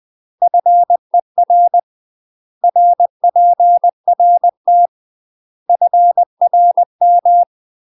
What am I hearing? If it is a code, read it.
Morse code, heard as a keyed tone, about 20 wpm, 700 Hz. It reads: FER RPRT FRM
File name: UK MASTER QSOs TRAINING / 20WPM / UK MASTER QSOs 0062